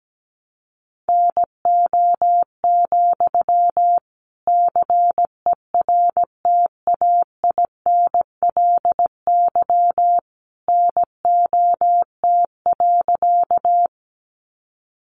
Morse code, heard as a keyed tone, about 17 wpm, 700 Hz. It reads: NO, CERTAINLY NOT.
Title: Sentences from Top 900 Words 17wpm